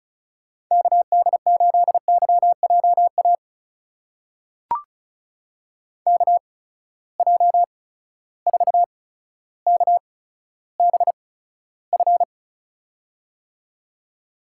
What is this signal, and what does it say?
Morse code, heard as a keyed tone, about 36 wpm, spaced as for 15 wpm, 700 Hz. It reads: KD8YJA KJ4KBF
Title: USA Call Signs - 8x Character Spacing - 35wpm